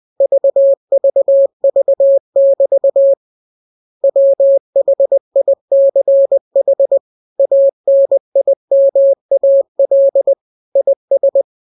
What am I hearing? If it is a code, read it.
Morse code, heard as a keyed tone, about 20 wpm, 550 Hz. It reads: VVV<BT> WHICH ANIMAL IS